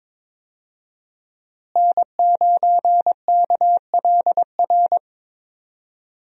Morse code, heard as a keyed tone, about 22 wpm, 700 Hz. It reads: N9KLR
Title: USA Call Signs - 5 Characters - Repeated 3x - 22wpm